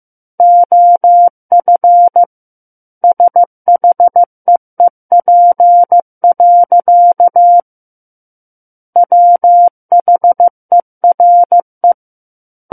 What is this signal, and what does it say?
Morse code, heard as a keyed tone, about 15 wpm, 700 Hz. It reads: OF SHEEP. WHERE
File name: Book One / 06-Chapter 6 - 15 WPM